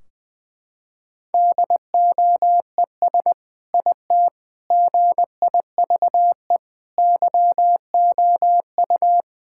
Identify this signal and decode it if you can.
Morse code, heard as a keyed tone, about 20 wpm, 700 Hz. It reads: DOES IT GIVE YOU
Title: Sentences from Top 1600 Words 20wpm